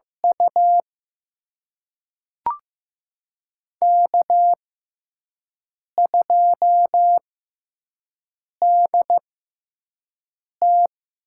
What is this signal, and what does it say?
Morse code, heard as a keyed tone, about 15 wpm, spaced as for 5 wpm, 700 Hz. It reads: U K2DT
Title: USA Call Signs - 4 Characters - 6x Character Spacing - 15wpm